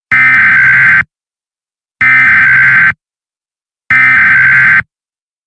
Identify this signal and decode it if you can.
Emergency Alert System (SAME) tones (header bursts), heard as harsh buzzing bursts. It is heard from 0.1 to 4.8 s.
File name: Emergency Alert System Eas Tones Attention Signals High Quality